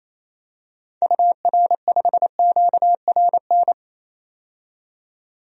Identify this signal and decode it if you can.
Morse code, heard as a keyed tone, about 28 wpm, 700 Hz. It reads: UR5QRN
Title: DXCC-100 Call Signs - 6 Characters - 28wpm